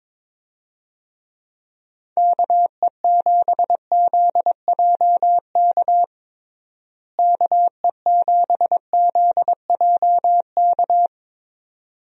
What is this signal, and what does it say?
Morse code, heard as a keyed tone, about 22 wpm, 700 Hz. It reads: KE7ZJK KE7ZJK
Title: USA Call Signs - 6 Characters - Repeated 2x - 22wpm